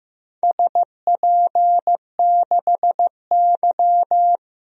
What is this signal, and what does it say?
Morse code, heard as a keyed tone, about 15 wpm, 700 Hz. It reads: SP6Y
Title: DXCC-100 Call Signs - 4 Characters - Repeated 2x - 15wpm